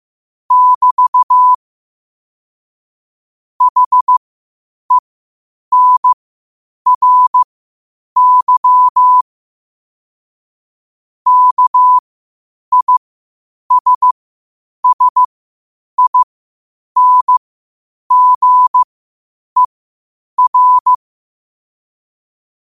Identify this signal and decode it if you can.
Morse code, heard as a keyed tone, about 15 wpm, spaced as for 8 wpm, 1000 Hz. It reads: <BT> HENRY KISSINGER
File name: qotdfiles / 8wpm